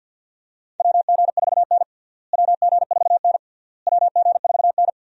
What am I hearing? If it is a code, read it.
Morse code, heard as a keyed tone, about 53 wpm, 700 Hz. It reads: WG4N WG4N WG4N
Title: USA Call Signs - 4 Characters - Repeated 3x - 50wpm